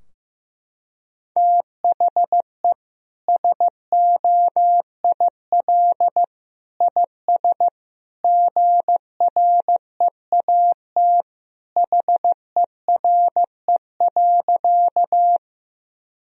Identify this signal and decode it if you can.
Morse code, heard as a keyed tone, about 15 wpm, 700 Hz. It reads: THE SOIL IS GREAT HERE.
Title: Sentences from Top 1000 Words - Review Part 4 - 15wpm